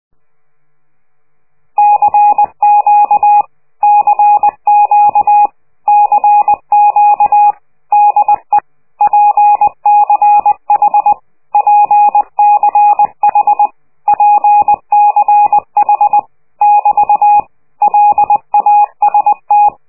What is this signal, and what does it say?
Morse code, heard as a keyed tone, about 20 wpm, 830 Hz. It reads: CQ CQ CQ DE PCH PCH PCH <BT> LAST
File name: worldcoastal / scheveningenradio / morsepch